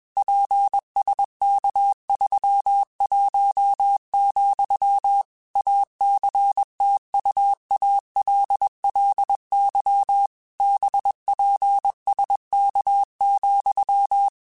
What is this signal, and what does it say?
Morse code, heard as a keyed tone, about 21 wpm, 800 Hz. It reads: PSK31, ACTUALLY BPSK,
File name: QSOs / QSOs-mp3s / kc7haa@20